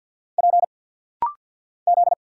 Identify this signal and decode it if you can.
Morse code, heard as a keyed tone, about 47 wpm, 700 Hz. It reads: P Z
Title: Single Letter-Number Alt - Rapid-Fire 50wpm